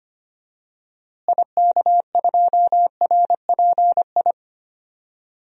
Morse code, heard as a keyed tone, about 25 wpm, 700 Hz. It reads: IK2RPI
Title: DXCC-100 Call Signs - 6 Characters - Repeated 2x - 25wpm